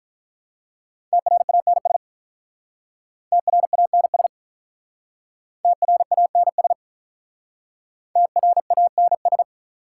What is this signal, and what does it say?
Morse code, heard as a keyed tone, about 46 wpm, 700 Hz. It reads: TRANS TRANS TRANS TRANS